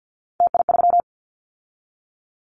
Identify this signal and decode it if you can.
Morse code, heard as a keyed tone, about 50 wpm, 700 Hz. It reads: TI3
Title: Callsign Prefixes for Top 100 DXCC - 50wpm